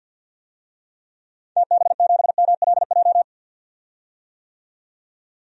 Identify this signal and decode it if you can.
Morse code, heard as a keyed tone, about 49 wpm, 700 Hz. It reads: TB7MLJ
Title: DXCC-100 Call Signs - 6 Characters - 50wpm